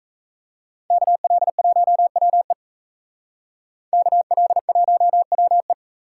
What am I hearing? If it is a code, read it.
Morse code, heard as a keyed tone, about 34 wpm, 700 Hz. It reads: KL1WE KL1WE
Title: Speed Racing - USA 2x2 Call Signs 28wpm